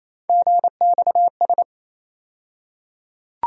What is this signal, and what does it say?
Morse code, heard as a keyed tone, about 28 wpm, 700 Hz. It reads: GXS
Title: USA Call Sign Postfixes - 3 Letters 28wpm